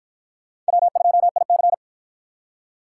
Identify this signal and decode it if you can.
Morse code, heard as a keyed tone, about 48 wpm, 700 Hz. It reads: W2IC